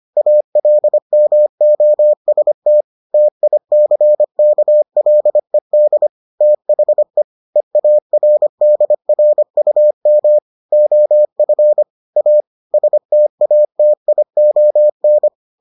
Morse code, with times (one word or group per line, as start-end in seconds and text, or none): 0.2-2.8 ALMOST
3.1-6.1 TICKLED
6.4-7.2 THE
7.6-10.4 EARDRUM
10.7-11.8 OF
12.2-12.4 A
12.7-15.3 STATION